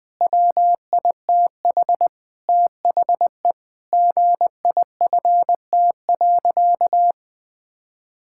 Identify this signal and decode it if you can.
Morse code, heard as a keyed tone, about 20 wpm, 700 Hz. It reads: WITH THE GIFT.
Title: Sentences with 5-Word Phrases - Spoken Beginning - 20wpm